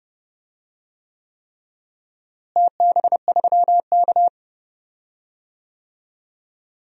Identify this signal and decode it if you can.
Morse code, heard as a keyed tone, about 30 wpm, 700 Hz. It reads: TB3K
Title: DXCC-100 Call Signs - 4 Characters - Repeated 3x - 30wpm